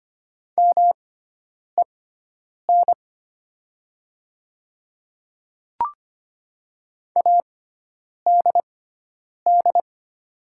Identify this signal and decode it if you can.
Morse code, heard as a keyed tone, about 25 wpm, spaced as for 9 wpm, 700 Hz. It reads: MEN ADD
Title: 6x Character Spacing - Top 200 Words - 25wpm